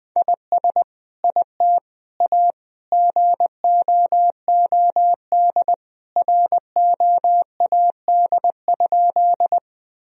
Morse code, heard as a keyed tone, about 20 wpm, 700 Hz. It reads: IS IT A GOOD ROAD?
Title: Sentences from Top 1000 Words - Review Part 2 - 20wpm